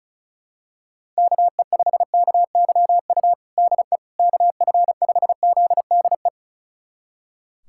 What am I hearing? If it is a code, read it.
Morse code, heard as a keyed tone, about 36 wpm, 700 Hz. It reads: KE5KYU DE KF5ZDE
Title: USA Callsign DE Callsign 35wpm